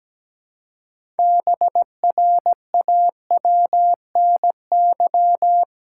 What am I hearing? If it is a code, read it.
Morse code, heard as a keyed tone, about 17 wpm, 700 Hz. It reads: BRAWNY
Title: Pangram Challenge v2 - 17wpm